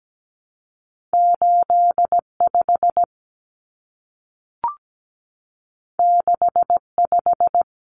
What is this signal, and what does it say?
Morse code, heard as a keyed tone, about 17 wpm, 700 Hz. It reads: 85 65
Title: Common Cut Numbers - 2-Digit 17wpm